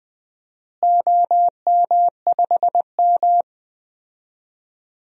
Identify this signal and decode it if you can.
Morse code, heard as a keyed tone, about 20 wpm, 700 Hz. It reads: OM5M
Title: DXCC-100 Call Signs - 4 Characters - Repeated 2x - 20wpm